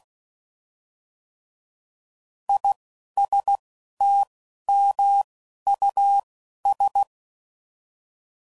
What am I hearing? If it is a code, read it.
Morse code, heard as a keyed tone, about 16 wpm, 800 Hz. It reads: I S T M U S
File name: Prefixes / By First Letter / Is